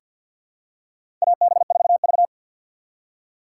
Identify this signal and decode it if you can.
Morse code, heard as a keyed tone, about 50 wpm, 700 Hz. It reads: AB4V
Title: USA Call Signs - 4 Characters - 3x Character Spacing - 50wpm